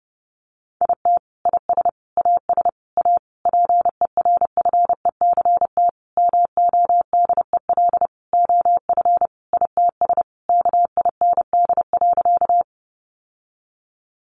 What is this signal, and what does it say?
Morse code, heard as a keyed tone, about 28 wpm, 700 Hz. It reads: IT IS AS A PERFECT MODEL OF ITS KIND.
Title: Sentences from Top 1000 Words 30wpm